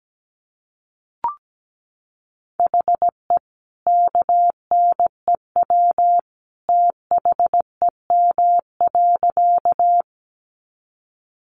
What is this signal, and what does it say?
Morse code, heard as a keyed tone, about 17 wpm, 700 Hz. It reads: HE KNEW THEM.
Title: Sentences from Top 400 Words 17wpm